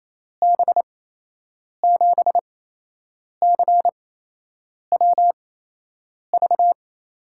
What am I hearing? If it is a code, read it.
Morse code, heard as a keyed tone, about 28 wpm, spaced as for 8 wpm, 700 Hz. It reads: B7CWV
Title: USA Call Signs - 6 Characters - 8x Character Spacing - 28wpm